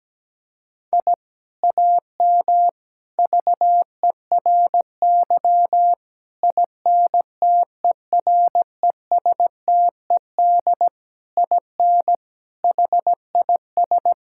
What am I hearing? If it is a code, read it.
Morse code, heard as a keyed tone, about 17 wpm, 700 Hz. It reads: I AM VERY INTERESTED IN HIS
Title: Sentences from Top 1000 Words - Review Part 6 - 17wpm